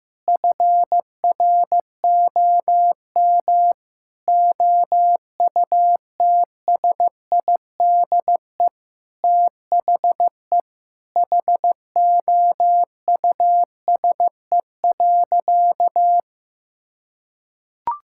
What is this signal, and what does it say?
Morse code, heard as a keyed tone, about 15 wpm, 700 Hz. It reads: FROM OUTSIDE THE HOUSE.
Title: Sentences from Top 1000 Words - Review Part 6 - 15wpm